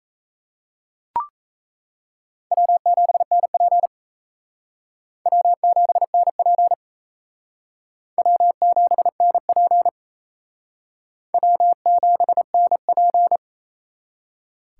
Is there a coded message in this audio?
Yes: Morse code from 2.5 to 13.4 s, heard as a keyed tone, about 36 wpm, 700 Hz. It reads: W7NP W7NP W7NP W7NP